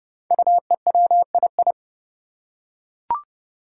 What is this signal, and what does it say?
Morse code, heard as a keyed tone, about 30 wpm, 700 Hz. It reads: UEWII
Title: Five Challenging Letter-Number Combinations 30wpm